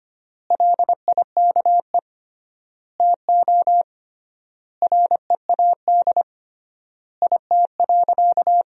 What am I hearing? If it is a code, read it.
Morse code, heard as a keyed tone, about 25 wpm, 700 Hz. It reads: LIKE TO READ IT.
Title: 3x Word-Spacing - Sentences from Top 500 Words - 25wpm